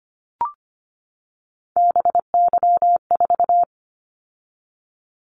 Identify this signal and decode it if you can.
Morse code, heard as a keyed tone, about 25 wpm, 700 Hz. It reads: BY4
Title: Callsign Prefixes for Top 100 DXCC - 25wpm